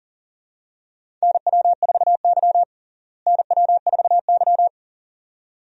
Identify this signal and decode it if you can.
Morse code, heard as a keyed tone, about 40 wpm, 700 Hz. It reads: NW4Y NW4Y
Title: USA Call Signs - 4 Characters - Repeated 2x - 40wpm